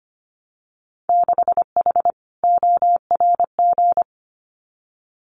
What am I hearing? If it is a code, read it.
Morse code, heard as a keyed tone, about 25 wpm, 700 Hz. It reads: 6H ORG
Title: Winter Field Day Exchanges Encore 25wpm